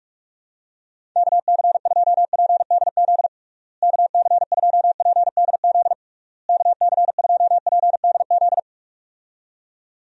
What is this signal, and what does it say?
Morse code, heard as a keyed tone, about 41 wpm, 700 Hz. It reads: KC2PDZ KC2PDZ KC2PDZ
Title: USA Call Signs - 6 Characters - Repeated 3x - 45wpm